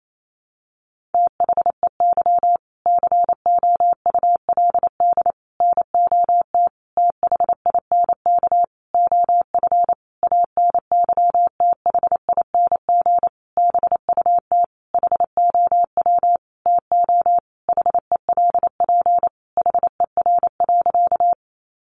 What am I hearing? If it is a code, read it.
Morse code, heard as a keyed tone, about 28 wpm, 700 Hz. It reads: THEY COULD NOT THINK OF ANYTHING BUT HOW TO HELP HER.
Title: Sentences from Top 1000 Words 28wpm